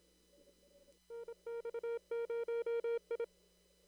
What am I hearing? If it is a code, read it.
Morse code, heard as a keyed tone, about 27 wpm, 470 Hz. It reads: NX0I
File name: CW / kt5j wpx cw-op-k5pi